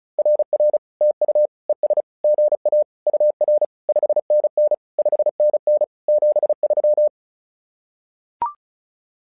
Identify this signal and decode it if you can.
Morse code, heard as a keyed tone, about 36 wpm, 600 Hz. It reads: RR TU ES GA UR 5NN 5NN 73
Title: SOTA QSO Practice 35wpm